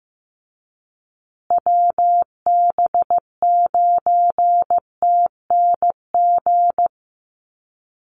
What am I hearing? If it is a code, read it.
Morse code, heard as a keyed tone, about 15 wpm, 700 Hz. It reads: WB9TNG
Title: USA Call Signs Repeated 3x 15wpm